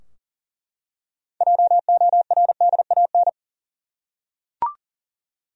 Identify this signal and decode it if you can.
Morse code, heard as a keyed tone, about 40 wpm, 700 Hz. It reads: JORDAN
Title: USA Common Names - 40wpm